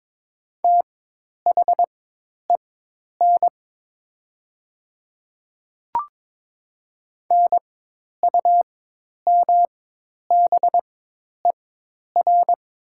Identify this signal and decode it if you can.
Morse code, heard as a keyed tone, about 22 wpm, spaced as for 10 wpm, 700 Hz. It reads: THEN NUMBER
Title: 4x Character Spacing - Top 100 Words - 22wpm